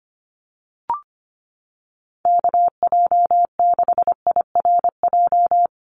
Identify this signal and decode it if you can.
Morse code, heard as a keyed tone, about 25 wpm, 700 Hz. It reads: KJ6IRJ
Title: USA Call Signs 25wpm